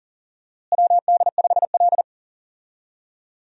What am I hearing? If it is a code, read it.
Morse code, heard as a keyed tone, about 40 wpm, 700 Hz. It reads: WD5L
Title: USA Call Signs - 4 Characters - Repeated 2x - 40wpm